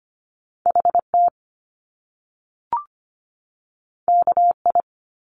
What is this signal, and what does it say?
Morse code, heard as a keyed tone, about 25 wpm, 700 Hz. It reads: HT KI
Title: Top Two Letter Combinations in Words 25wpm